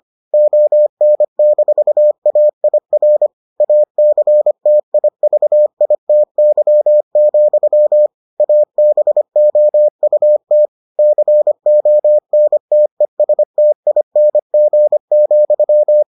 Morse code, heard as a keyed tone, about 25 wpm, 600 Hz. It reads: ON-AIR ACTIVITY, ABOUT CONTESTING,